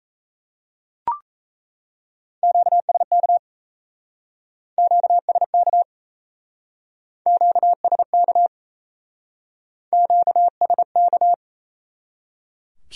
Morse code, heard as a keyed tone, about 35 wpm, 700 Hz. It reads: QSK QSK QSK QSK
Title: Speed-Racing Q-Codes - 28wpm